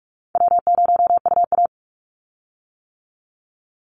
Morse code, heard as a keyed tone, about 44 wpm, 700 Hz. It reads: W0UA
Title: Easy Call Signs - 45wpm